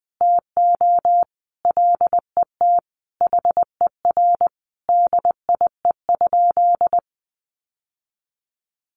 Morse code, heard as a keyed tone, about 19 wpm, 700 Hz. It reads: TO LET HER DIE?
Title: Sentences from Top 1300 Words 20wpm